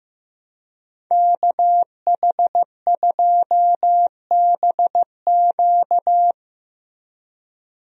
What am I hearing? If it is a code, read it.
Morse code, heard as a keyed tone, about 15 wpm, 700 Hz. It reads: KH2BQ
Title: USA Call Signs - 5 Characters - Repeated 2x - 15wpm